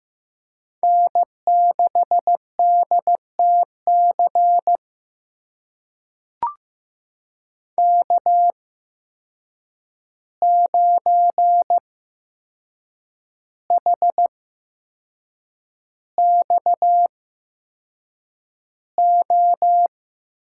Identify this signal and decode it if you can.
Morse code, heard as a keyed tone, about 15 wpm, spaced as for 6 wpm, 700 Hz. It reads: N6DTC K9HXO
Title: USA Call Signs - 5 Characters - 8x Character Spacing - 15wpm